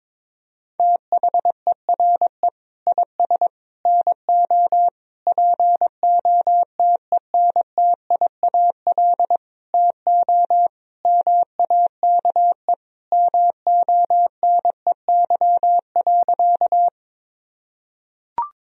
Morse code, heard as a keyed tone, about 22 wpm, 700 Hz. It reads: THERE IS NO POTENTIAL TO MAKE MONEY.
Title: Sentences from Top 1000 Words - Review Part 4 - 22wpm